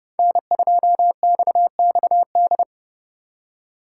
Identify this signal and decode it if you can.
Morse code, heard as a keyed tone, about 39 wpm, 700 Hz. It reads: N2XXD